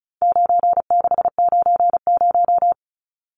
Morse code, heard as a keyed tone, about 34 wpm, 700 Hz. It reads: 9690
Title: Numbers - 4-Digit 35wpm